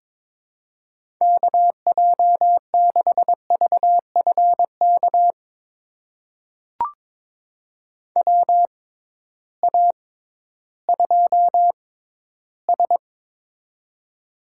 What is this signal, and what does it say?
Morse code, heard as a keyed tone, about 22 wpm, spaced as for 13 wpm, 700 Hz. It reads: KJ6VFK WA2S
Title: USA Call Signs - 6x Character Spacing - 22wpm